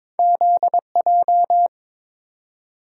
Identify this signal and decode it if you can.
Morse code, heard as a keyed tone, about 22 wpm, 700 Hz. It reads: ZJ
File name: practice / 22wpm / Two Letter-Number Combinations - 22wpm